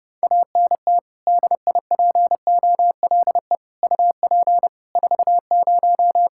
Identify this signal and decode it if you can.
Morse code, heard as a keyed tone, about 30 wpm, 700 Hz. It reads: ANT DIPOLE UP 40
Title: Practice QSO - Exchange 2 - Info - 30wpm